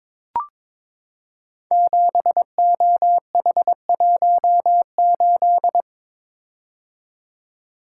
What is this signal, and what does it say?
Morse code, heard as a keyed tone, about 22 wpm, 700 Hz. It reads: 7OH18
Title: Code Groups - 5 Letter:Numbers - 22wpm